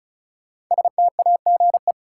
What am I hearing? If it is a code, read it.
Morse code, heard as a keyed tone, about 36 wpm, 700 Hz. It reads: STAGE
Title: Sentences from Top 1200 Words - New Word Drill Alt - 35wpm